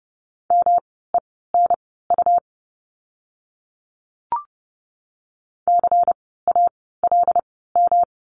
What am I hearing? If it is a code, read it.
Morse code, heard as a keyed tone, about 30 wpm, spaced as for 17 wpm, 700 Hz. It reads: MENU CALM
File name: course / 30:10wpm / 108 Lesson 18f - Letter m - Words 30wpm